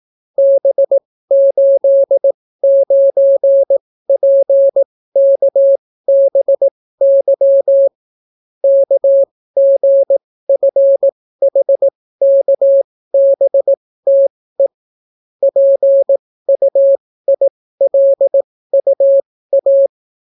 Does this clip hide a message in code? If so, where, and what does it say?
Yes: Morse code from 0.4 to 19.9 s, heard as a keyed tone, about 18 wpm, spaced as for 15 wpm, 550 Hz. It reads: B89PKBY KGFHKBTE PUILUA